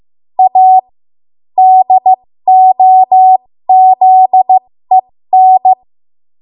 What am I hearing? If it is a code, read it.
Morse code, heard as a keyed tone, about 15 wpm, spaced as for 13 wpm, 750 Hz. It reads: A DOZEN
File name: Morse / 13WPM